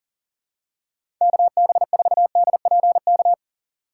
Morse code, heard as a keyed tone, about 40 wpm, 700 Hz. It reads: KB4DPK
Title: USA Call Signs - 6 Characters - 2x Character Spacing - 40wpm